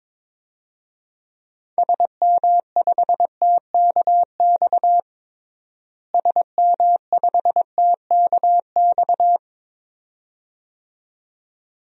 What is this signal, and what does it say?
Morse code, heard as a keyed tone, about 23 wpm, 700 Hz. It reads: SM5TKX SM5TKX
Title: DXCC-100 Call Signs - 6 Characters - Repeated 2x - 22wpm